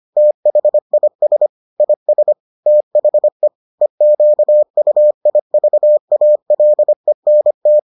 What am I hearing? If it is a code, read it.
Morse code, heard as a keyed tone, about 25 wpm, 600 Hz. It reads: THIS IS THE EQUIVALENT